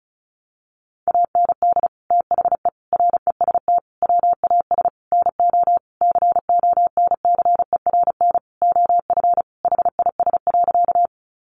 Morse code, heard as a keyed tone, about 34 wpm, 700 Hz. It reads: AND THE REST WAS NO CONCERN OF HIS.
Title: Sentences from Top 1000 Words 35wpm